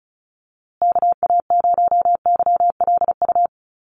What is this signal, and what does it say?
Morse code, heard as a keyed tone, about 34 wpm, 700 Hz. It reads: KA0YLU
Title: USA Call Signs Repeated 2x 35wpm